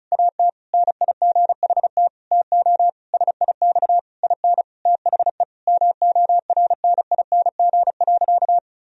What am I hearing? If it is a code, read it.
Morse code, heard as a keyed tone, about 34 wpm, 700 Hz. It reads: AT NIGHT TO SIX IN THE MORNING.
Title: Sentences from Top 1000 Words - Review Part 2 - 35wpm